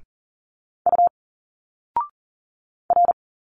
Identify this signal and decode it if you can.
Morse code, heard as a keyed tone, about 41 wpm, 700 Hz. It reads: U R
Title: Single Letters 40wpm